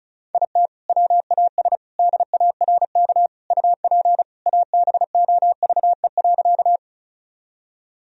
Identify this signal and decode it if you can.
Morse code, heard as a keyed tone, about 36 wpm, 700 Hz. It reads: IT WAS DARK UP ABOVE.
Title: Sentences from Top 1000 Words - Review Part 2 - 35wpm